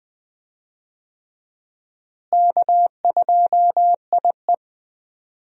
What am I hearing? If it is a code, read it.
Morse code, heard as a keyed tone, about 20 wpm, 700 Hz. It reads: K2IE
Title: USA Call Signs - 4 Characters - Repeated 3x - 20wpm